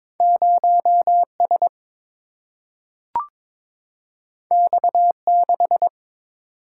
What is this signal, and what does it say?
Morse code, heard as a keyed tone, about 22 wpm, 700 Hz. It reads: 0S X6
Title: Single-Letter Single-Number 22wpm